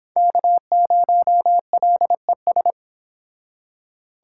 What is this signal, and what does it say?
Morse code, heard as a keyed tone, about 30 wpm, 700 Hz. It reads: K0LES